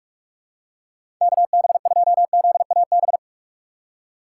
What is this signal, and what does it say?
Morse code, heard as a keyed tone, about 46 wpm, 700 Hz. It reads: KB2ZAB
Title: USA Call Signs - 6 Characters - 6x Character Spacing - 45wpm